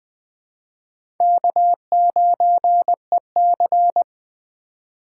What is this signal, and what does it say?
Morse code, heard as a keyed tone, about 20 wpm, 700 Hz. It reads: K9EC
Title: USA Call Signs - 4 Characters - 3x Character Spacing - 20wpm